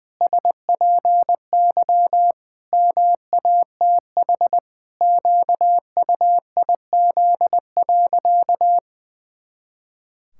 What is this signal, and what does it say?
Morse code, heard as a keyed tone, about 20 wpm, 700 Hz. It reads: SPY MATH QUIZ.
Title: Pangram Challenge v2 - 20wpm